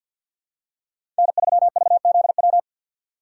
Morse code, heard as a keyed tone, about 52 wpm, 700 Hz. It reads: N3VZW
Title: USA Call Signs - 5 Characters - 3x Character Spacing - 50wpm